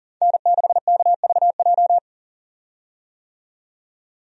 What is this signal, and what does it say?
Morse code, heard as a keyed tone, about 39 wpm, 700 Hz. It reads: N6KVJ
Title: USA Call Signs - 5 Characters - 40wpm